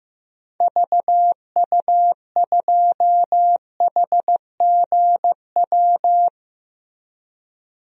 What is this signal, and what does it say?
Morse code, heard as a keyed tone, about 15 wpm, 700 Hz. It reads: VU2HGW
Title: DXCC-100 Call Signs - 6 Characters - Repeated 2x - 15wpm